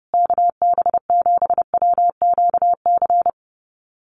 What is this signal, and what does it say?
Morse code, heard as a keyed tone, about 30 wpm, 700 Hz. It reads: KB7WQC
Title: USA Call Signs Repeated 2x 30wpm